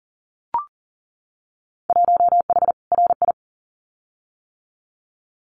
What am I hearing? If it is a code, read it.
Morse code, heard as a keyed tone, about 40 wpm, 700 Hz. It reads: 1H RI
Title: Winter Field Day Exchanges 40wpm